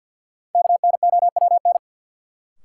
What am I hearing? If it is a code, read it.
Morse code, heard as a keyed tone, about 52 wpm, 700 Hz. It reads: KNOWN
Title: Sentences from Top 600 Words - New Word Drill - 50wpm